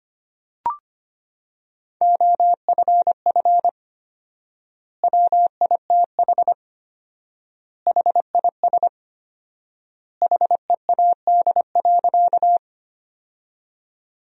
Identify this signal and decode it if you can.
Morse code, heard as a keyed tone, about 25 wpm, 700 Hz. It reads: OFF WITH HIS HEAD.
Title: 4x Word-Spacing - Sentences from Top 300 Words - 25wpm